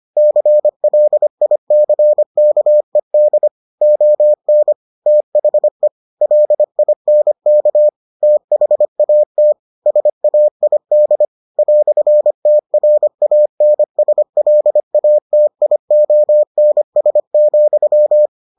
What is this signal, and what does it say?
Morse code, heard as a keyed tone, about 25 wpm, 600 Hz. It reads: CLICKED ON THE LINK THAT SAID "TRANSLATIONS,